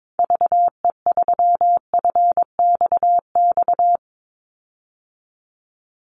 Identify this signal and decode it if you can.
Morse code, heard as a keyed tone, about 22 wpm, 700 Hz. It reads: VE3FXX
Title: Call Signs Repeated 2x - 22wpm